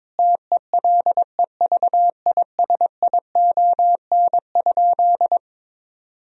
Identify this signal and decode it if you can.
Morse code, heard as a keyed tone, about 22 wpm, 700 Hz. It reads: TELEVISION?
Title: Sentences from Top 1000 Words - Review Part 4 - 22wpm